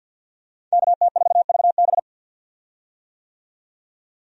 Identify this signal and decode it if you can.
Morse code, heard as a keyed tone, about 50 wpm, 700 Hz. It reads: KT4VB
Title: USA Call Signs - 5 Characters - 50wpm